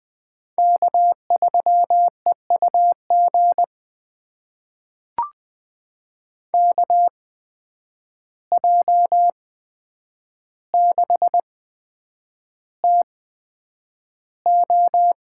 Morse code, heard as a keyed tone, about 20 wpm, spaced as for 9 wpm, 700 Hz. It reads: K3EUG KJ6TO
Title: USA Call Signs - 5 Characters - 8x Character Spacing - 20wpm